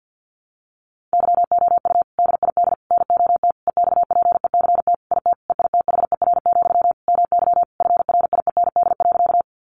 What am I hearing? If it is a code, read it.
Morse code, heard as a keyed tone, about 50 wpm, 700 Hz. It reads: YOU DID NOT EXPECT IT EITHER, MY FRIEND.
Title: Sentences from Top 1100 Words 50wpm